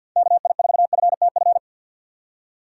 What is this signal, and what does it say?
Morse code, heard as a keyed tone, about 52 wpm, 700 Hz. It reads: KI4FTF
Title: USA Call Signs - 6 Characters - 3x Character Spacing - 50wpm